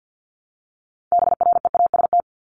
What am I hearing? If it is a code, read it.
Morse code, heard as a keyed tone, about 48 wpm, 700 Hz. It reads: BREAST
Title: Top 1751-2000 Words 50wpm